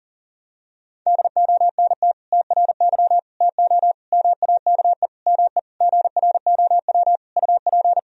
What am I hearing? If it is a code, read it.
Morse code, heard as a keyed tone, about 40 wpm, 700 Hz. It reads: DONT TRY TO MAKE ME GROW UP
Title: Sentences from Top 1000 Words - Review Part 1 - 40wpm